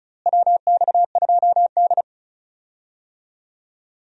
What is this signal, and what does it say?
Morse code, heard as a keyed tone, about 34 wpm, 700 Hz. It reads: WX2D
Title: USA Call Signs - 4 Characters - Repeated 2x - 35wpm